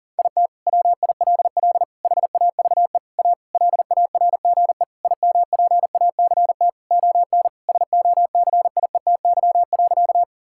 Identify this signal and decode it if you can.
Morse code, heard as a keyed tone, about 40 wpm, 700 Hz. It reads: IT WILL HAVE A LARGE IMPACT ON SOCIETY.
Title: Sentences from Top 1000 Words - Review Part 6 - 40wpm